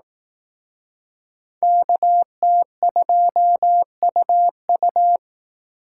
Morse code, heard as a keyed tone, about 18 wpm, 700 Hz. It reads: KT2UU
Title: Speed Racing - USA Call Signs 15wpm